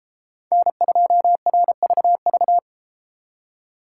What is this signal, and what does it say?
Morse code, heard as a keyed tone, about 30 wpm, 700 Hz. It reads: N2RVV